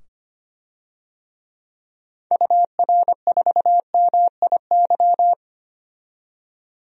Morse code, heard as a keyed tone, about 25 wpm, 700 Hz. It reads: UR4MIY
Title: DXCC-100 Call Signs - 6 Characters - 25wpm